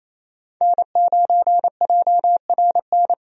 Morse code, heard as a keyed tone, about 28 wpm, 700 Hz. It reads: N9JRN
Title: USA Call Signs - 5 Characters - 28wpm